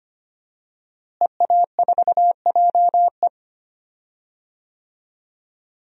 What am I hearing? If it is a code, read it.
Morse code, heard as a keyed tone, about 25 wpm, 700 Hz. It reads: EA4JE
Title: DXCC-100 Call Signs - 5 Characters - Repeated 2x - 25wpm